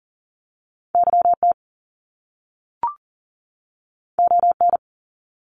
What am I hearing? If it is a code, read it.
Morse code, heard as a keyed tone, about 40 wpm, 700 Hz. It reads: YT ON
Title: Canadian Provinces and Territories 40wpm